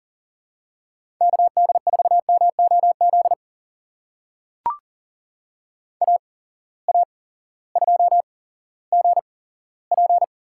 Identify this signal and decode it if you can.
Morse code, heard as a keyed tone, about 41 wpm, spaced as for 18 wpm, 690 Hz. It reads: KD4MOZ AA2GP
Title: USA Call Signs - 8x Character Spacing - 40wpm